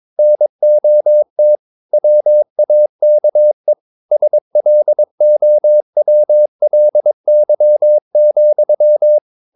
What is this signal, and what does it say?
Morse code, heard as a keyed tone, about 22 wpm, 600 Hz. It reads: NOT WAKE SLOWLY,
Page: Romanen Never af Ken Follet på engelsk.